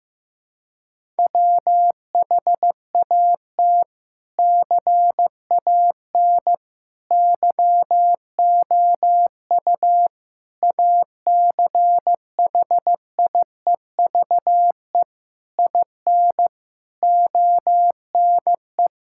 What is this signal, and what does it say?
Morse code, heard as a keyed tone, about 16 wpm, 700 Hz. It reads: WHAT CAN YOU ACHIEVE IN ONE
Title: Sentences from Top 1000 Words - Review Part 6 - 15wpm